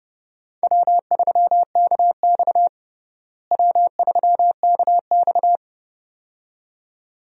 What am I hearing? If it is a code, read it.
Morse code, heard as a keyed tone, about 30 wpm, 700 Hz. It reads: W3KX W3KX
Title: USA Call Signs - 4 Characters - Repeated 2x - 30wpm